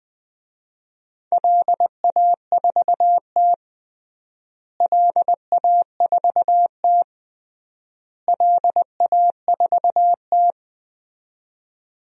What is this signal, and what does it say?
Morse code, heard as a keyed tone, about 20 wpm, 700 Hz. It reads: LA4T LA4T LA4T
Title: DXCC-100 Call Signs - 4 Characters - Repeated 3x - 20wpm